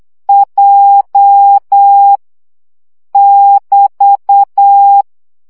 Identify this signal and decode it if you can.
Morse code, heard as a keyed tone, about 8 wpm, 800 Hz. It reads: J <BT>
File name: CW gyakorlas j 7 baud